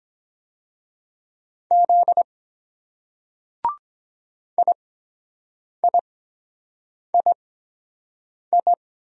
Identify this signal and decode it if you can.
Morse code, heard as a keyed tone, about 22 wpm, spaced as for 7 wpm, 700 Hz. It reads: Z IIII